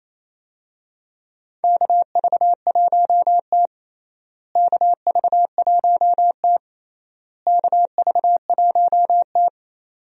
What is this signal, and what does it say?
Morse code, heard as a keyed tone, about 28 wpm, 700 Hz. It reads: KV1T KV1T KV1T
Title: USA Call Signs - 4 Characters - Repeated 3x - 28wpm